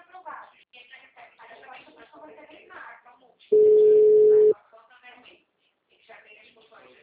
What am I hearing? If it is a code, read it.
Ringback tone (Europe, 425 Hz), with speech.